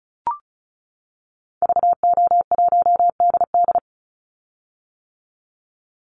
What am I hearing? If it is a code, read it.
Morse code, heard as a keyed tone, about 34 wpm, 700 Hz. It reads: VO1DD
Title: Call Signs - 35wpm